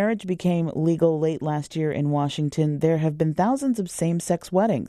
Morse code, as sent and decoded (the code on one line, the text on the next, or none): none